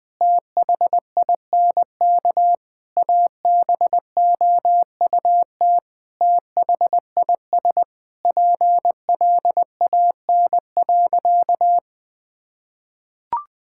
Morse code, heard as a keyed tone, about 21 wpm, 700 Hz. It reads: THINK ABOUT THIS PLAN.
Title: Sentences with 9-Word Phrases - 20wpm